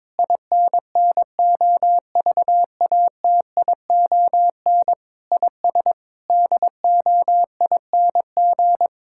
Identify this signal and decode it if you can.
Morse code, heard as a keyed tone, about 22 wpm, 700 Hz. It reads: INNOVATION IS DOING
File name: Difficult Sentences with Hints - 22wpm